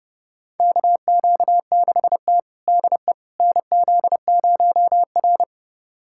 Morse code, heard as a keyed tone, about 30 wpm, 700 Hz. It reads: KQ6T DE NZ0R
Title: USA Callsign DE Callsign - 4 Characters - 30wpm